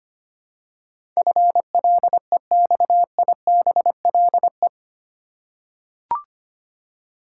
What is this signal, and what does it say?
Morse code, heard as a keyed tone, about 25 wpm, 700 Hz. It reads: FLEXIBLE
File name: 8 Letter Words - First 3 Letters Given - 25wpm